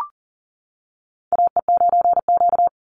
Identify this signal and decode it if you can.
Morse code, heard as a keyed tone, about 40 wpm, 700 Hz. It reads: AE9Q
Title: NA SOTA Chaser Callsigns 40wpm